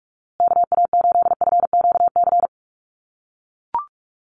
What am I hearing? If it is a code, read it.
Morse code, heard as a keyed tone, about 44 wpm, 700 Hz. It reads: KA8FQC
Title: USA Call Signs Repeated 3x 45wpm